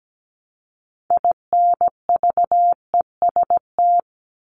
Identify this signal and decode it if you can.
Morse code, heard as a keyed tone, about 17 wpm, 700 Hz. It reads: INVEST
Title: Top 1001-1250 Words 17wpm